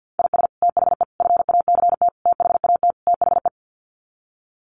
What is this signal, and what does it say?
Morse code, heard as a keyed tone, about 50 wpm, 700 Hz. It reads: IS THE FACT THAT THE
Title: Sets of 5 Words Encore2 50wpm